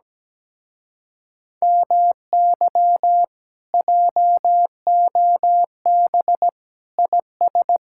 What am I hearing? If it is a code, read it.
Morse code, heard as a keyed tone, about 17 wpm, 700 Hz. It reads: MY JOB IS
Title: Speed-Racing Common QSO Phrases - 17wpm